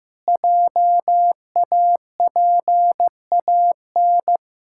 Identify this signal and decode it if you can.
Morse code, heard as a keyed tone, about 15 wpm, 700 Hz. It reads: JAPAN
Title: Pangram Challenge v2 - 15wpm